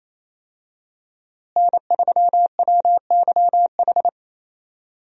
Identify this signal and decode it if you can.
Morse code, heard as a keyed tone, about 27 wpm, 700 Hz. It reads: N3WYH
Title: USA Call Signs - 5 Characters - Repeated 3x - 28wpm